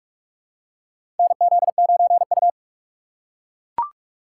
Morse code, heard as a keyed tone, about 44 wpm, 700 Hz. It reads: NZ9U
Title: USA Call Signs - 4 Characters - 2x Character Spacing - 45wpm